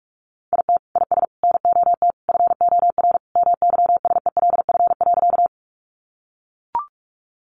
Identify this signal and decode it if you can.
Morse code, heard as a keyed tone, about 44 wpm, 700 Hz. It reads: IT IS NOT FOR MYSELF.
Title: Sentences from Top 1300 Words 45wpm